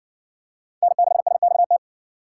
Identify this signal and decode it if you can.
Morse code, heard as a keyed tone, about 60 wpm, 700 Hz. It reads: N6SXT